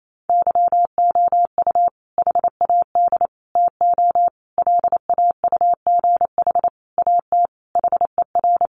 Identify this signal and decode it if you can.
Morse code, heard as a keyed tone, about 28 wpm, 700 Hz. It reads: YOU HAD TO LAUGH AT HER
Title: Sentences from Top 600 Words 28wpm